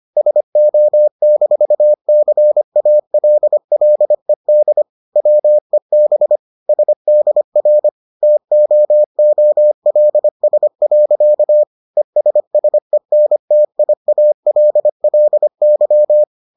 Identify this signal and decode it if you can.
Morse code, heard as a keyed tone, about 25 wpm, 600 Hz. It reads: SO-CALLED WEB SDR TOOLS. ESSENTIALLY